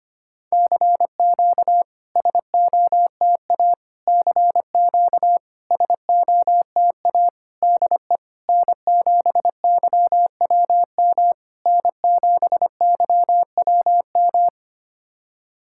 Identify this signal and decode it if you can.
Morse code, heard as a keyed tone, about 25 wpm, 700 Hz. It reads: CQ SOTA CQ SOTA DE N7YWM N7YWM
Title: SOTA QSO Practice 25wpm